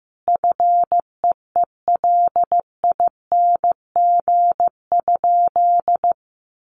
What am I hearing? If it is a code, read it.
Morse code, heard as a keyed tone, about 15 wpm, 700 Hz. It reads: FEELING?
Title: Sentences from Top 1200 Words 15wpm